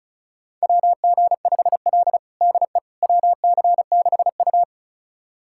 Morse code, heard as a keyed tone, about 36 wpm, 700 Hz. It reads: WG5L DE WC6U
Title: USA Callsign DE Callsign - 4 Characters - 35wpm